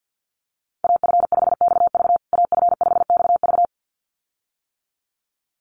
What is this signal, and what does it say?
Morse code, heard as a keyed tone, about 50 wpm, 700 Hz. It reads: AF5XV AF5XV
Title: USA Call Signs Repeated 2x 50wpm